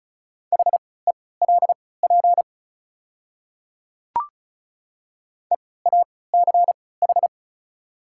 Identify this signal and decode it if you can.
Morse code, heard as a keyed tone, about 34 wpm, spaced as for 20 wpm, 700 Hz. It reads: HELP EACH
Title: 3x Character Spacing - Top 300 Words - 35wpm